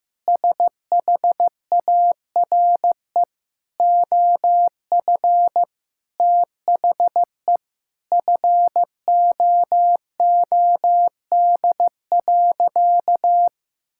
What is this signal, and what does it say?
Morse code, heard as a keyed tone, about 15 wpm, 700 Hz. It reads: SHARE OF THE FOOD.
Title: Sentences from Top 1000 Words - Review Part 6 - 15wpm